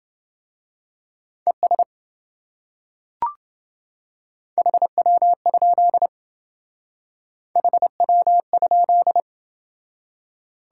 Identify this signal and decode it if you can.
Morse code, heard as a keyed tone, about 27 wpm, 700 Hz. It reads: ES HW? HW?